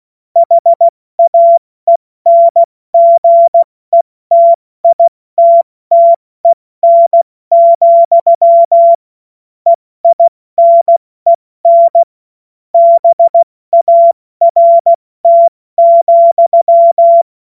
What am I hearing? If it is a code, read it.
Morse code, heard as a keyed tone, about 16 wpm, 680 Hz. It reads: HAENGETITTEN, EINEN BART,
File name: Morsekurs / Witze / Witz 07. März 2023 70 F80